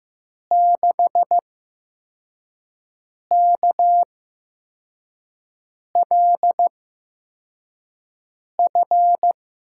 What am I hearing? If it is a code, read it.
Morse code, heard as a keyed tone, about 15 wpm, spaced as for 4 wpm, 700 Hz. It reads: 6KLF
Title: USA Call Signs - 6 Characters - 8x Character Spacing - 15wpm